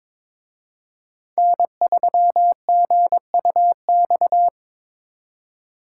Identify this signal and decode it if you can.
Morse code, heard as a keyed tone, about 26 wpm, 700 Hz. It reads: N3GUX
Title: Speed Racing - USA Call Signs 22wpm